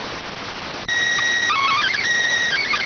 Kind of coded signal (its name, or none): SSTV (slow-scan television)